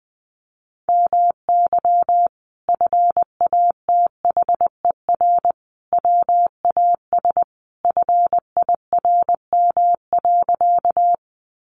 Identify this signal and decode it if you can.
Morse code, heard as a keyed tone, about 20 wpm, 700 Hz. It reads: MY FATHER WAS FIRM.
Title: Sentences from Top 1400 Words 20wpm